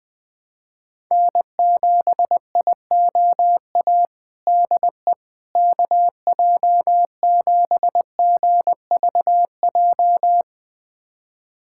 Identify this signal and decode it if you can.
Morse code, heard as a keyed tone, about 20 wpm, 700 Hz. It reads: N7IOA DE KJ7GVJ
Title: USA Callsign DE Callsign 20wpm